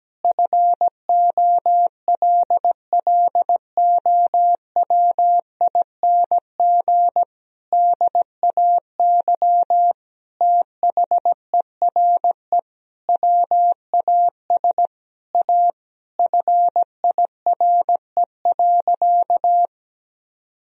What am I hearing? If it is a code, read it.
Morse code, heard as a keyed tone, about 17 wpm, 700 Hz. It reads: FOLLOWING DAY THERE WAS A FIRE.
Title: Sentences from Top 1000 Words - Review Part 4 - 17wpm